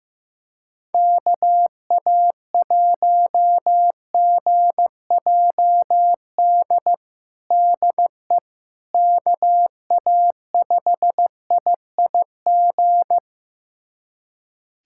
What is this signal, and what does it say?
Morse code, heard as a keyed tone, about 15 wpm, 700 Hz. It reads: KA1GJD DE KA5IIG
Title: USA Callsign DE Callsign - 6 Characters - 15wpm